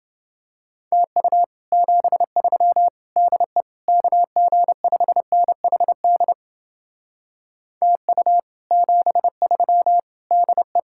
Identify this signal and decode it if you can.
Morse code, heard as a keyed tone, about 29 wpm, 700 Hz. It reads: TU 73 DE KG5NHD TU 73 DE